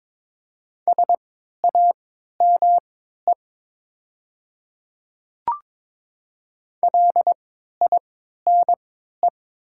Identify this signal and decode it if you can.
Morse code, heard as a keyed tone, about 21 wpm, spaced as for 12 wpm, 700 Hz. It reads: SAME LINE
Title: 3x Character Spacing - Top 500 Words - 22wpm